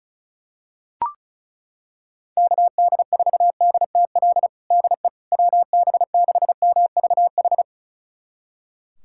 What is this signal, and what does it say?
Morse code, heard as a keyed tone, about 36 wpm, 700 Hz. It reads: KD4DTL DE WB6MVH
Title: USA Callsign DE Callsign - 6 Characters - 35wpm